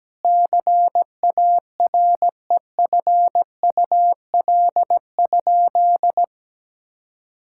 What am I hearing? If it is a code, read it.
Morse code, heard as a keyed tone, about 17 wpm, 700 Hz. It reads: CAREFUL?
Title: Sentences from Top 1600 Words 17wpm